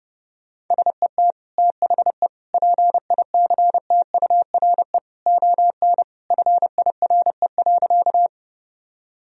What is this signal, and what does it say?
Morse code, heard as a keyed tone, about 30 wpm, 700 Hz. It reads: SET THE PICTURE ON FIRE.
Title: Sentences from Top 1000 Words - Review Part 1 - 30wpm